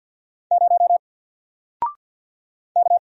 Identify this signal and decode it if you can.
Morse code, heard as a keyed tone, about 48 wpm, 700 Hz. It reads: 0 K
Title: Single Character (Letter:Number:Punctuation:Prosign) 50wpm